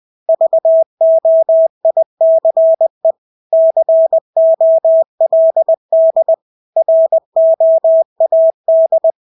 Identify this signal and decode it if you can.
Morse code, heard as a keyed tone, about 20 wpm, 650 Hz. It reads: VOICE COLD ROAD
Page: Engelske ord 20 wpm